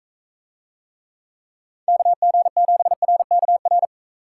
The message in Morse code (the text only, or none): KG7RKR